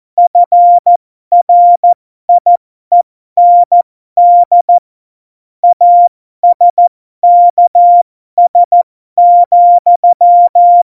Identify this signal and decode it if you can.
Morse code, heard as a keyed tone, about 14 wpm, spaced as for 12 wpm, 700 Hz. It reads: FRIEND ASKS,
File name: Morsekurs / Witze